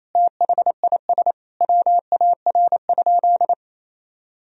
Morse code, heard as a keyed tone, about 28 wpm, 700 Hz. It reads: THIS WAR?
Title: Sentences from Top 1000 Words - Review Part 2 - 28wpm